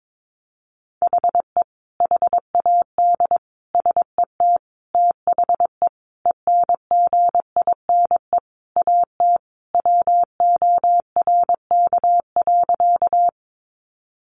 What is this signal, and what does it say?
Morse code, heard as a keyed tone, about 22 wpm, 700 Hz. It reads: HE HAD SET THE ENGINE AT WORK.
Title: Sentences from Top 1500 Words 22wpm